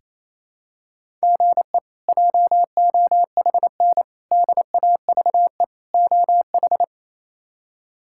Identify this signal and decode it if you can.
Morse code, heard as a keyed tone, about 28 wpm, 700 Hz. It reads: GE JOHN DAVE OH
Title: K1USN SST QSO Practice - 28wpm